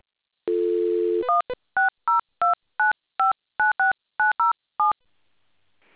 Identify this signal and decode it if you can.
DTMF, heard as telephone touch tones, tones of about 140 ms. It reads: 16029596907